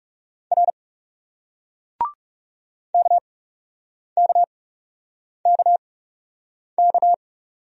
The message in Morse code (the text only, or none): R KKKK